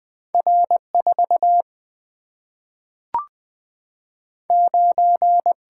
Morse code, heard as a keyed tone, about 20 wpm, 700 Hz. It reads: R 4 9
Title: Single-Letter Single-Number 20wpm